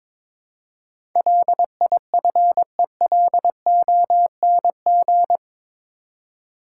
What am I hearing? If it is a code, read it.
Morse code, heard as a keyed tone, about 21 wpm, 700 Hz. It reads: LIFELONG
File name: practice / 22wpm / Eight Letter Words - 22wpm